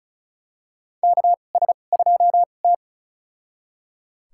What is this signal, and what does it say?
Morse code, heard as a keyed tone, about 34 wpm, spaced as for 25 wpm, 700 Hz. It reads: KS2T
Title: USA Call Signs - 4 Characters - 2x Character Spacing - 35wpm